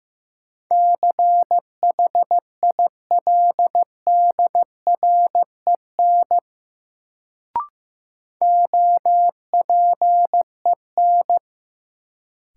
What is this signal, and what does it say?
Morse code, heard as a keyed tone, about 15 wpm, 700 Hz. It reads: CHILDREN OPEN
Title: Sentences from Top 300 Words - New Word Drill - 15wpm